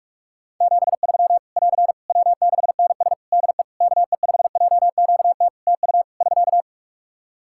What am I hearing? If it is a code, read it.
Morse code, heard as a keyed tone, about 46 wpm, 700 Hz. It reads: 73 <AR> W6NS DE KE5JQT TU <SK>
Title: Practice QSO - Exchange 4 - Ending - 45wpm